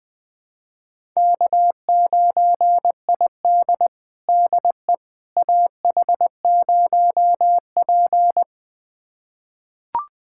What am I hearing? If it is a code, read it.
Morse code, heard as a keyed tone, about 20 wpm, 700 Hz. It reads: K9ID DE AH0P
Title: USA Callsign DE Callsign - 4 Characters - 20wpm